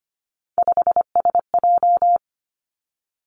Morse code, heard as a keyed tone, about 25 wpm, 700 Hz. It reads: 5SJ
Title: Three Challenging Letter-Number Combinations 25wpm